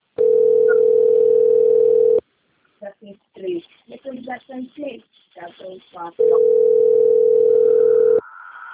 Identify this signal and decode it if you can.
Ringback tone (North America), 440 + 480 Hz, with speech.
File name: rg-1600-unknown- - - .52623